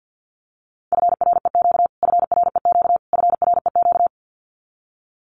Morse code, heard as a keyed tone, about 50 wpm, 700 Hz. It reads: FREQ FREQ FREQ
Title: Abbreviations Repeated 3x - 50wpm